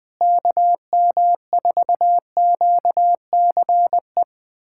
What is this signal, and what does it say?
Morse code, heard as a keyed tone, about 20 wpm, 700 Hz. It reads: KM4QCE
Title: USA Callsign DE Callsign 20wpm